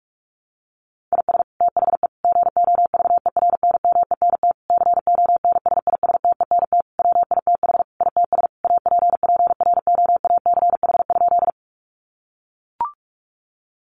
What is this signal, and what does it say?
Morse code, heard as a keyed tone, about 44 wpm, 700 Hz. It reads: IS THE GOVERNMENT CONSISTENT WITH ITS APPROACH?
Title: Sentences from Top 1500 Words 45wpm